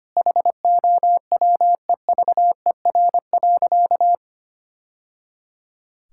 Morse code, heard as a keyed tone, about 25 wpm, 700 Hz. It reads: HOWEVER.
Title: Sentences from Top 1000 Words - Review Part 6 - 25wpm